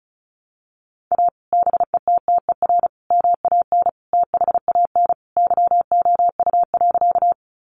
Morse code, heard as a keyed tone, about 34 wpm, 700 Hz. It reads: A BETTER MAN THAN YOU.
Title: Sentences from Top 400 Words 35wpm